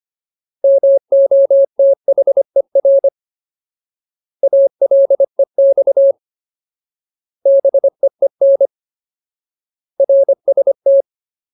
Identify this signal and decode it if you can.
Morse code, heard as a keyed tone, about 25 wpm, 550 Hz. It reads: MOTHER ALEX BEEN RST
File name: Ultimate Word List / 25WPM Ultimate